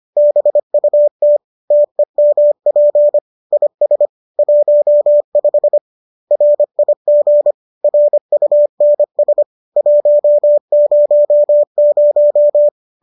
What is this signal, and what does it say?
Morse code, heard as a keyed tone, about 25 wpm, 590 Hz. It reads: BUT TEMP IS 15 RIG RUNS 100
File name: qso221 25